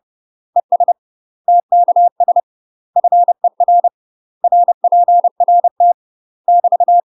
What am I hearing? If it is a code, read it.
Morse code, heard as a keyed tone, about 30 wpm, 700 Hz. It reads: ES TKS FER RPRT <BT>
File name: UK MASTER QSOs TRAINING / 30WPM / UK MASTER QSOs 0048